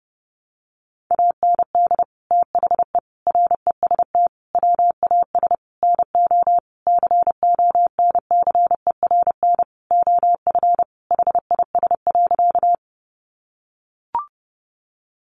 Morse code, heard as a keyed tone, about 30 wpm, 700 Hz. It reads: AND THE REST WAS NO CONCERN OF HIS.
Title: Sentences from Top 1000 Words 30wpm